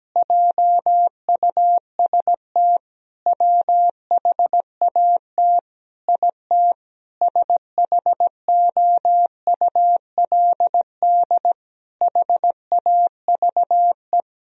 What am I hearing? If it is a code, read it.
Morse code, heard as a keyed tone, about 17 wpm, 700 Hz. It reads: JUST WHAT IT SHOULD HAVE
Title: Sentences from Top 1000 Words - Review Part 4 - 17wpm